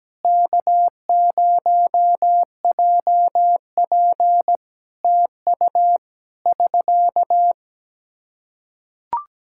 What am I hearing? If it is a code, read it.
Morse code, heard as a keyed tone, about 17 wpm, 700 Hz. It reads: K0JP TU <SK>
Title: Practice QSO - Exchange 4 - Ending - 17wpm